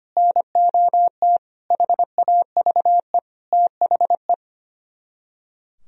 Morse code, heard as a keyed tone, about 33 wpm, 700 Hz. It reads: NOT HAVE THE